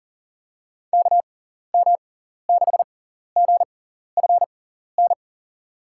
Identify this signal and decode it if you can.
Morse code, heard as a keyed tone, about 40 wpm, spaced as for 14 wpm, 700 Hz. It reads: KM6GFN
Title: USA Call Signs - 6 Characters - 6x Character Spacing - 40wpm